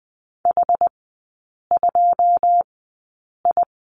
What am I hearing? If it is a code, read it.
Morse code, heard as a keyed tone, about 20 wpm, spaced as for 8 wpm, 690 Hz. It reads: H2I
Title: 081 Lesson 14e - Number 2 - Three Characters 20wpm